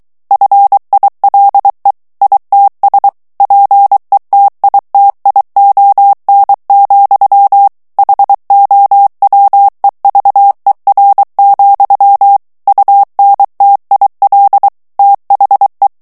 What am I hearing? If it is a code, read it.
Morse code, heard as a keyed tone, about 23 wpm, 800 Hz. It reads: FILE ITS PETITION, HOWEVER, UNTIL THE
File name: ANGLAIS texte clair 3 vitesse 22 mots minute